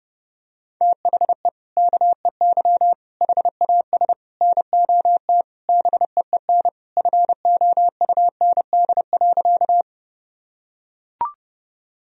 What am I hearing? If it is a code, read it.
Morse code, heard as a keyed tone, about 30 wpm, 700 Hz. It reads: THE KEY HAS NOT BEEN FOUND.
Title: Sentences from Top 1000 Words - Review Part 4 - 30wpm